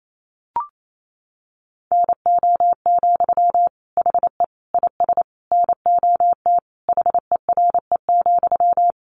Morse code, heard as a keyed tone, about 28 wpm, 700 Hz. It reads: NO, HE IS NOT HERE,
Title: Sentences from Top 1400 Words 28wpm